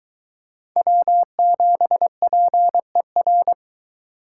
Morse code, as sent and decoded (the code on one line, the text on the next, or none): .-- --... .--. .-.
W7PER